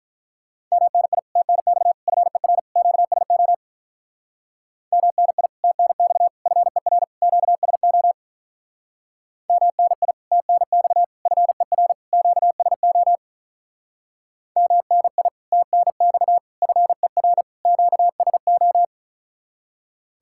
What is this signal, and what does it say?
Morse code, heard as a keyed tone, about 46 wpm, 700 Hz. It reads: MNI TNX FER QSO MNI TNX FER QSO MNI TNX FER QSO MNI TNX FER QSO